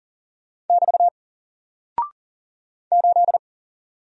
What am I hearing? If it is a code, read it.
Morse code, heard as a keyed tone, about 40 wpm, 700 Hz. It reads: <BT> 8
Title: Single Character (Letter:Number:Punctuation:Prosign) Alt 40wpm